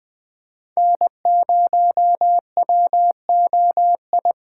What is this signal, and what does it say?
Morse code, heard as a keyed tone, about 21 wpm, 700 Hz. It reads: N0WOI
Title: K1USN SST QSO Practice - 20wpm